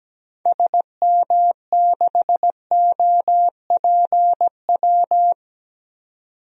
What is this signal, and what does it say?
Morse code, heard as a keyed tone, about 17 wpm, 700 Hz. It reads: SM6OPW
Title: DXCC-100 Call Signs - 6 Characters - 17wpm